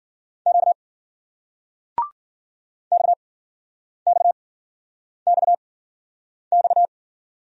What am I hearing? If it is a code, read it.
Morse code, heard as a keyed tone, about 55 wpm, 700 Hz. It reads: Q X X X X